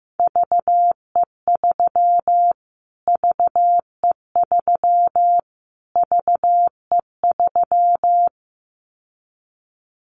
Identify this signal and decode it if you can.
Morse code, heard as a keyed tone, about 15 wpm, 700 Hz. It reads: VE3 VE3 VE3
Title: Callsign Prefixes for Top 25 DXCC - Repeated 3x - 15wpm